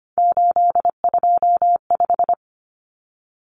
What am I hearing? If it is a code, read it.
Morse code, heard as a keyed tone, about 25 wpm, 700 Hz. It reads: 825
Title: Common Cut Numbers - 3-Digit 25wpm